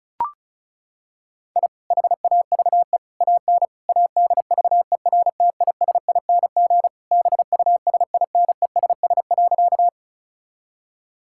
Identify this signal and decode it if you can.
Morse code, heard as a keyed tone, about 34 wpm, 700 Hz. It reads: I HAVE AN ADVERTISING BUSINESS.
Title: Sentences from Top 1000 Words - Review Part 6 - 35wpm